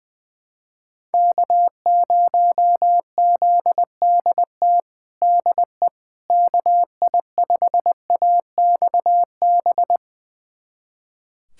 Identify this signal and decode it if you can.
Morse code, heard as a keyed tone, about 20 wpm, 700 Hz. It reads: K0ZDT DE KI5AXB
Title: USA Callsign DE Callsign 20wpm